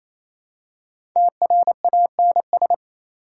Morse code, heard as a keyed tone, about 28 wpm, 700 Hz. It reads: TRANS
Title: Prefixes and Suffixes 28wpm